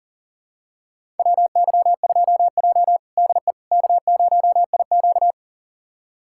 Morse code, heard as a keyed tone, about 40 wpm, 700 Hz. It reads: WY2J DE K0IQ
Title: USA Callsign DE Callsign - 4 Characters - 40wpm